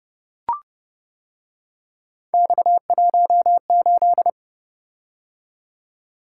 Morse code, heard as a keyed tone, about 29 wpm, 700 Hz. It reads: X18